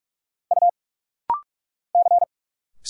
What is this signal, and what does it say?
Morse code, heard as a keyed tone, about 46 wpm, 700 Hz. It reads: U C
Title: Single Letters - Rapid-Fire 45wpm